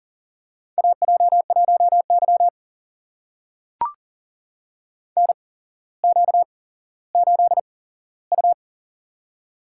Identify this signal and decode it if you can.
Morse code, heard as a keyed tone, about 39 wpm, spaced as for 17 wpm, 700 Hz. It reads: AJ1Y NQ8U
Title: USA Call Signs - 4 Characters - 8x Character Spacing - 40wpm